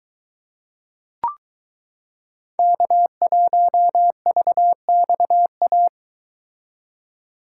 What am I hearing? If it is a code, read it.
Morse code, heard as a keyed tone, about 22 wpm, 700 Hz. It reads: K1VXA